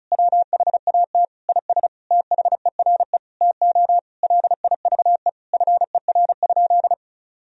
Morse code, heard as a keyed tone, about 34 wpm, 690 Hz. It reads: WHAT IS THERE TO LIVE FER?
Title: Sentences from Top 200 Words and Abbreviations 35wpm